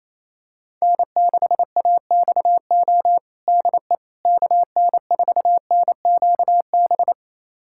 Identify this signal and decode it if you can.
Morse code, heard as a keyed tone, about 28 wpm, 700 Hz. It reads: N6AXO DE KN4NQB
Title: USA Callsign DE Callsign 28wpm